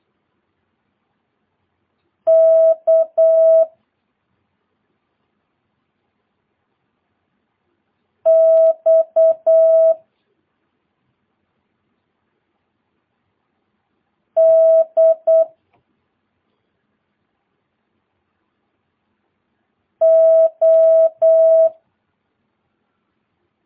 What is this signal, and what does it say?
Morse code, heard as a keyed tone, about 8 wpm, 650 Hz. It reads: K X D O
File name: jh8cbh / 2022cw